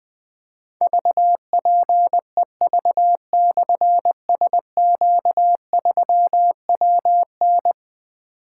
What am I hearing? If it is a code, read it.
Morse code, heard as a keyed tone, about 20 wpm, 700 Hz. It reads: VPEV/SQ3WN
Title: Difficult Call Signs 20wpm